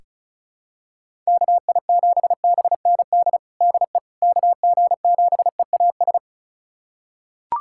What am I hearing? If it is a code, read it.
Morse code, heard as a keyed tone, about 36 wpm, 700 Hz. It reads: KI7BND DE KG7EAS
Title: USA Callsign DE Callsign - 6 Characters - 35wpm